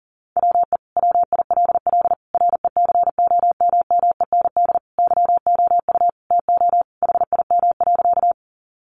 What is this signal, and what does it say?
Morse code, heard as a keyed tone, about 40 wpm, 700 Hz. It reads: WE WILL RECOMMEND YOU TO HIM.
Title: Sentences from Top 1200 Words 40wpm